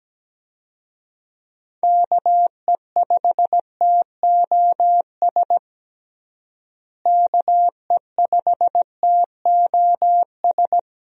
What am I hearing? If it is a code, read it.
Morse code, heard as a keyed tone, about 17 wpm, 700 Hz. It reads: KE5TOS KE5TOS
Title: USA Call Signs - 6 Characters - Repeated 2x - 17wpm